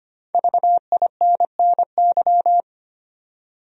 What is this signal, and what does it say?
Morse code, heard as a keyed tone, about 25 wpm, 700 Hz. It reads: VINNY
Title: USA Common Names - Max 5 Letters - 25wpm